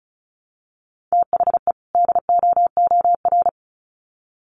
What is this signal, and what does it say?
Morse code, heard as a keyed tone, about 34 wpm, 700 Hz. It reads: THE DOOR
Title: Sets of 2 Words 35wpm